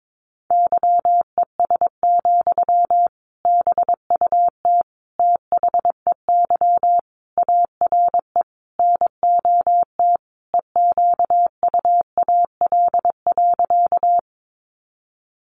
Morse code, heard as a keyed tone, about 22 wpm, 700 Hz. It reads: YES, BUT THEY ARE NOT EQUAL.
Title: Sentences from Top 1400 Words 22wpm